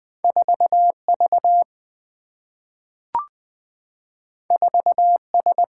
Morse code, heard as a keyed tone, about 20 wpm, 700 Hz. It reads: 4 V 4 S
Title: Single-Letter Single-Number 20wpm